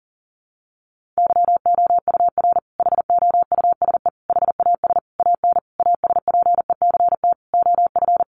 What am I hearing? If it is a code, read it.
Morse code, heard as a keyed tone, about 40 wpm, 700 Hz. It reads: YOUR HOUSE HAS AN ASPECT OF
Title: Sentences from Top 1300 Words 40wpm